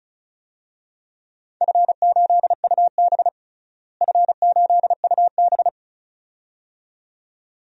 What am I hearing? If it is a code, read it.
Morse code, heard as a keyed tone, about 36 wpm, 700 Hz. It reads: F8UB F8UB
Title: DXCC-100 Call Signs - 4 Characters - Repeated 2x - 35wpm